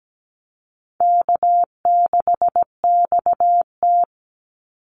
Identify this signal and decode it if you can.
Morse code, heard as a keyed tone, about 17 wpm, 700 Hz. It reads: K6XT
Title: Easy Call Signs - 17wpm